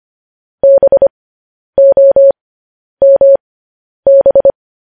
Morse code, heard as a keyed tone, about 25 wpm, spaced as for 10 wpm, 550 Hz. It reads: BOMB
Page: news-headlines-in-morse-code-at-10-wpm